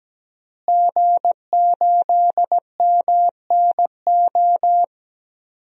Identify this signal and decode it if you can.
Morse code, heard as a keyed tone, about 17 wpm, 700 Hz. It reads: G8MNO
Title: DXCC-100 Call Signs - 5 Characters - Repeated 2x - 17wpm